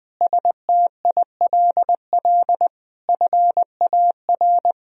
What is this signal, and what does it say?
Morse code, heard as a keyed tone, about 20 wpm, 700 Hz. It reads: STILL FAR
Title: Sentences from Top 1000 Words - Review Part 4 - 20wpm